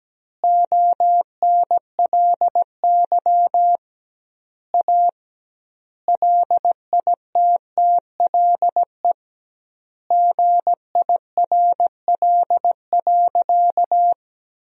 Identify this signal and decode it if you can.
Morse code, heard as a keyed tone, about 17 wpm, 700 Hz. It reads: ONLY A LITTLE GIRL.
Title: 2x Word-Spacing - Sentences from Top 400 Words - 17wpm